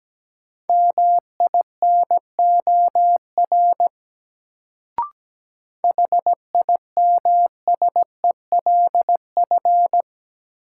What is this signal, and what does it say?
Morse code, heard as a keyed tone, about 17 wpm, 700 Hz. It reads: MINOR HIMSELF
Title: Sentences from Top 1500 Words - New Word Drill - 17wpm